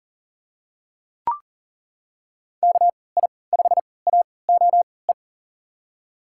Morse code, heard as a keyed tone, about 40 wpm, spaced as for 23 wpm, 700 Hz. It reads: KI5AOE
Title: USA Call Signs - 6 Characters - 3x Character Spacing - 40wpm